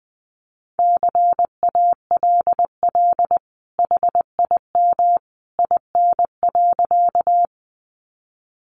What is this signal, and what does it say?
Morse code, heard as a keyed tone, about 20 wpm, 700 Hz. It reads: CALL HIM IN.
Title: Sentences from Top 100 Words 20wpm